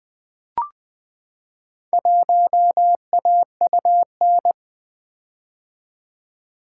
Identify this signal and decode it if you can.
Morse code, heard as a keyed tone, about 20 wpm, 700 Hz. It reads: 1AUN
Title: Four Letter-Number Combinations - 20wpm